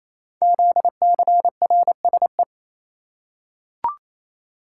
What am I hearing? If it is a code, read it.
Morse code, heard as a keyed tone, about 27 wpm, 700 Hz. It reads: ZCRSE
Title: Code Groups - 5 Letters - 28wpm